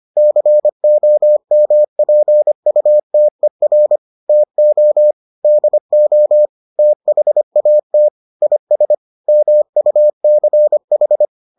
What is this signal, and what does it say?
Morse code, heard as a keyed tone, about 25 wpm, 600 Hz. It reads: COMPUTER TO DO THAT IS MUCH